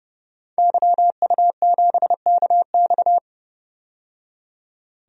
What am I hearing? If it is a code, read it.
Morse code, heard as a keyed tone, about 31 wpm, 700 Hz. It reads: YU7KX
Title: DXCC-100 Call Signs - 5 Characters - 30wpm